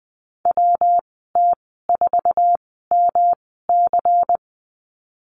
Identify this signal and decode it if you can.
Morse code, heard as a keyed tone, about 20 wpm, spaced as for 14 wpm, 700 Hz. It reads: WT4MC
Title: 193 Lesson 31g - Letter v - Call Signs 20wpm